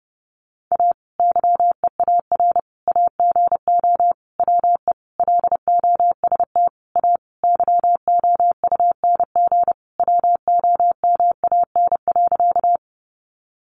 Morse code, heard as a keyed tone, about 30 wpm, 700 Hz. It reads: A YEAR AGO WE LOST A YOUNG WOMAN.
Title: Sentences from Top 1400 Words 30wpm